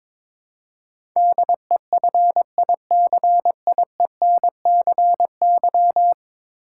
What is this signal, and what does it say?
Morse code, heard as a keyed tone, about 22 wpm, 700 Hz. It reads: DEFICIENCY
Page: morse-code-ninja-practice-22wpm